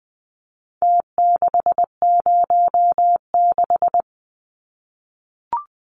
Morse code, heard as a keyed tone, about 20 wpm, 700 Hz. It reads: T606
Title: Numbers - Contest Style - 4-Digit 20wpm